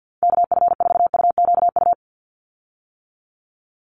Morse code, heard as a keyed tone, about 50 wpm, 700 Hz. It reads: KF4UQU
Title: USA Call Signs 50wpm